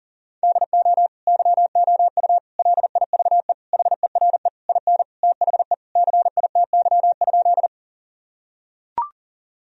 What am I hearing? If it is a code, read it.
Morse code, heard as a keyed tone, about 37 wpm, 700 Hz. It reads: DO YOU LIVE HERE IN THE CITY?
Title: Sentences from Top 1000 Words - Review Part 1 - 40wpm